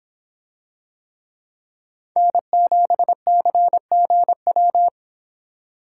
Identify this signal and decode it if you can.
Morse code, heard as a keyed tone, about 26 wpm, 700 Hz. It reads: N7CGW